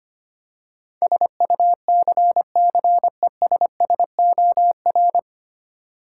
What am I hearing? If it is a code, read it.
Morse code, heard as a keyed tone, about 25 wpm, 700 Hz. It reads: SUCCESSOR
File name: Nine Letter Words - 25wpm